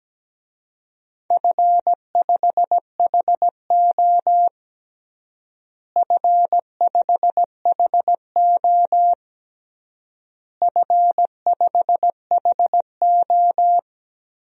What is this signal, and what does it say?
Morse code, heard as a keyed tone, about 17 wpm, 700 Hz. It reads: F5HO F5HO F5HO
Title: DXCC-100 Call Signs - 4 Characters - Repeated 3x - 17wpm